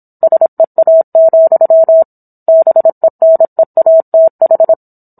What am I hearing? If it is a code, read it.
Morse code, heard as a keyed tone, about 27 wpm, 650 Hz. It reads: SEA, BENEATH